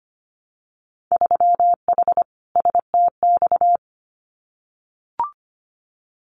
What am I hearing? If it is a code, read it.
Morse code, heard as a keyed tone, about 25 wpm, 700 Hz. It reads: 3H STX
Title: Winter Field Day Exchanges 25wpm